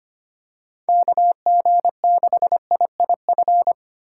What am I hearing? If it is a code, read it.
Morse code, heard as a keyed tone, about 25 wpm, 700 Hz. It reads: KG6IIF
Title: USA Call Signs - 6 Characters - 2x Character Spacing - 25wpm